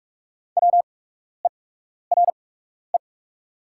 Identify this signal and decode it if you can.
Morse code, heard as a keyed tone, about 44 wpm, spaced as for 12 wpm, 690 Hz. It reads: WERE
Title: 8x Character Spacing - Top 200 Words - 45wpm